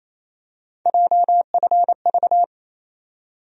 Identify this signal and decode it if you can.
Morse code, heard as a keyed tone, about 28 wpm, 700 Hz. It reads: JFV
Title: USA Call Sign Postfixes - 3 Letters 28wpm